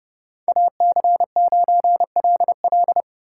Morse code, heard as a keyed tone, about 36 wpm, 700 Hz. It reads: AC9LL
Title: Speed Racing - USA 2x2 Call Signs 30wpm